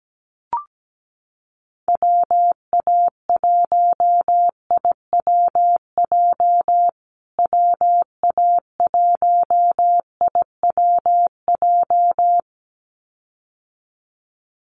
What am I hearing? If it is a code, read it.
Morse code, heard as a keyed tone, about 17 wpm, 700 Hz. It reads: WA1IWJ WA1IWJ
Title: USA Call Signs Repeated 2x 17wpm